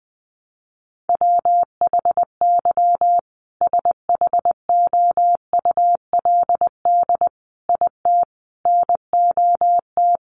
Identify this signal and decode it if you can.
Morse code, heard as a keyed tone, about 20 wpm, 700 Hz. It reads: WHY SHOULD IT NOT
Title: Sentences from Top 1400 Words 20wpm